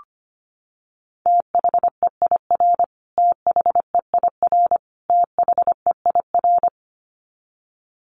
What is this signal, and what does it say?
Morse code, heard as a keyed tone, about 25 wpm, 700 Hz. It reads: THEIR THEIR THEIR
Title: Top 500 Words Repeated 25wpm